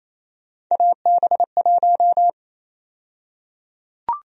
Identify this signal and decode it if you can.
Morse code, heard as a keyed tone, about 28 wpm, 700 Hz. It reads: AB1
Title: USA Call Sign Prefixes - 28wpm